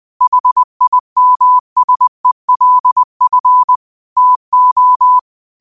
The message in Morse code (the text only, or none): HIMSELF TO